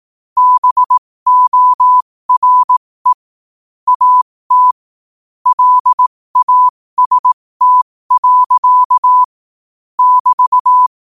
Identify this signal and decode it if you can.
Morse code, heard as a keyed tone, about 18 wpm, spaced as for 15 wpm, 1000 Hz. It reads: BORE AT LAST. <BT>
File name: qotdfiles / 15wpm